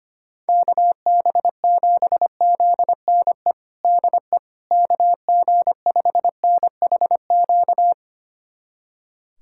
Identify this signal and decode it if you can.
Morse code, heard as a keyed tone, about 25 wpm, 700 Hz. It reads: KB7ZNE DE KG5NHQ
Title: USA Callsign DE Callsign 25wpm